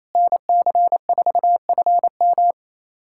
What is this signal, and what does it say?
Morse code, heard as a keyed tone, about 28 wpm, 700 Hz. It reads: NC4FM
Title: USA Callsign DE Callsign - 5 Characters - 28wpm